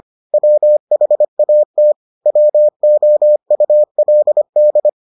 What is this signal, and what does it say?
Morse code, heard as a keyed tone, about 25 wpm, 600 Hz. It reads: WHAT WOULD